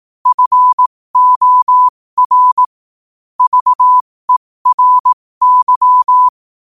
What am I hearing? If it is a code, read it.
Morse code, heard as a keyed tone, about 18 wpm, spaced as for 15 wpm, 1000 Hz. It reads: FOR VERY
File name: qotdfiles / 15wpm